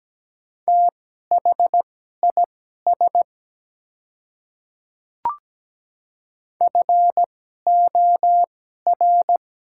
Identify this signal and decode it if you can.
Morse code, heard as a keyed tone, about 17 wpm, spaced as for 12 wpm, 700 Hz. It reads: THIS FOR
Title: 2x Character Spacing - Top 100 Words - 17wpm